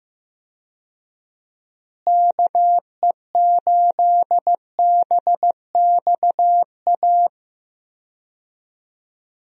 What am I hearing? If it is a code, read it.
Morse code, heard as a keyed tone, about 15 wpm, 700 Hz. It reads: KE8BXA
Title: USA Call Signs - 6 Characters - Repeated 3x - 15wpm